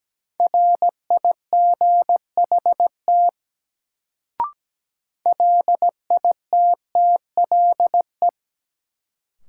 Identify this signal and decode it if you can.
Morse code, heard as a keyed tone, about 17 wpm, 700 Hz. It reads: RIGHT LITTLE
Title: Sentences from Top 200 Words - New Word Drill - 17wpm